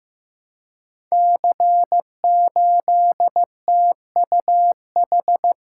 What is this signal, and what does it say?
Morse code, heard as a keyed tone, about 15 wpm, 700 Hz. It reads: C8TUH
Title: Code Groups - 5 Letter:Numbers - 15wpm